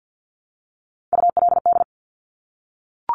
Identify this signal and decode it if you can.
Morse code, heard as a keyed tone, about 50 wpm, 700 Hz. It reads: ULD
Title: Top Three Letter Combinations in Words 50wpm